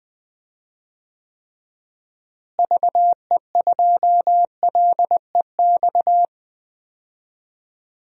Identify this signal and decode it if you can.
Morse code, heard as a keyed tone, about 21 wpm, 700 Hz. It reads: VE2LEX
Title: DXCC-100 Call Signs - 6 Characters - Repeated 2x - 20wpm